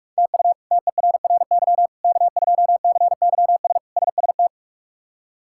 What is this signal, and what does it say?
Morse code, heard as a keyed tone, about 48 wpm, 700 Hz. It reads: TU TERRY K2CYS SST
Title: K1USN SST QSO Practice - 45wpm